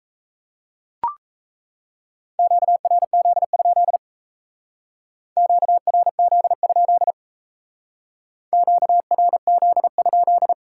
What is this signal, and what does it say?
Morse code, heard as a keyed tone, about 34 wpm, 700 Hz. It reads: QRZ? QRZ? QRZ?
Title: Speed-Racing Common QSO Phrases - 28wpm